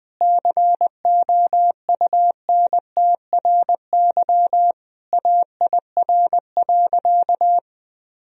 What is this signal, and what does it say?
Morse code, heard as a keyed tone, about 21 wpm, 700 Hz. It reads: COUNTRY AIR.
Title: Sentences from Top 1000 Words - Review Part 1 - 20wpm